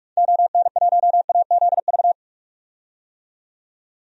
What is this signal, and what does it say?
Morse code, heard as a keyed tone, about 46 wpm, 700 Hz. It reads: ON1AZV
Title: DXCC-100 Call Signs - 6 Characters - 45wpm